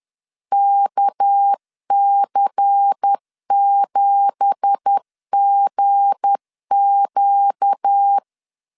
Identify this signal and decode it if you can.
Morse code, heard as a keyed tone, about 10 wpm, 800 Hz. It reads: KC7GQ